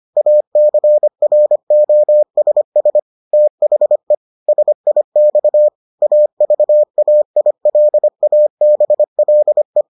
Morse code, heard as a keyed tone, about 25 wpm, 600 Hz. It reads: ACROSS THE SIX AVAILABLE